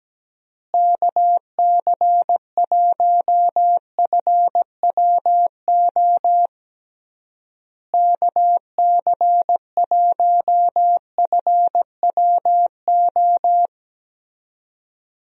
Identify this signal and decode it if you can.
Morse code, heard as a keyed tone, about 17 wpm, 700 Hz. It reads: KC1FWO KC1FWO
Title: USA Call Signs - 6 Characters - Repeated 3x - 17wpm